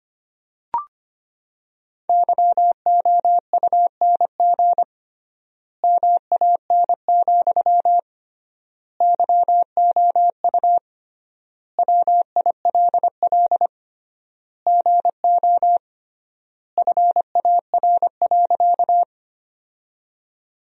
Morse code, heard as a keyed tone, about 25 wpm, 700 Hz. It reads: YOUNG MAN, YOU WILL GO FAR.
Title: 3x Word-Spacing - Sentences from Top 400 Words - 25wpm